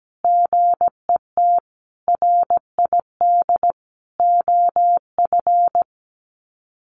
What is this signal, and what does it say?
Morse code, heard as a keyed tone, about 17 wpm, 700 Hz. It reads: GET RID OF
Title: Sets of 4 Words 17wpm